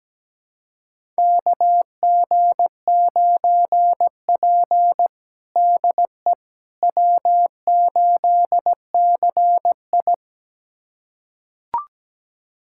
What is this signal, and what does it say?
Morse code, heard as a keyed tone, about 17 wpm, 700 Hz. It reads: KG9P DE W8CI
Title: USA Callsign DE Callsign - 4 Characters - 17wpm